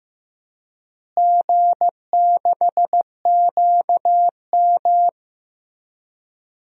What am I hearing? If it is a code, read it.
Morse code, heard as a keyed tone, about 15 wpm, 700 Hz. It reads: G6QM
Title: DXCC-100 Call Signs - 4 Characters - Repeated 2x - 15wpm